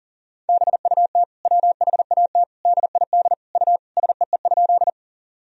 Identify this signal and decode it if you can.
Morse code, heard as a keyed tone, about 40 wpm, 700 Hz. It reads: BUT WHAT DID U SEE?
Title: Sentences from Top 100 Words and Abbreviations 40wpm